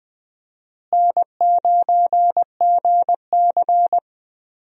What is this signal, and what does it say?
Morse code, heard as a keyed tone, about 20 wpm, 700 Hz. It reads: N9GC
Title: USA Call Signs - 4 Characters - Repeated 3x - 20wpm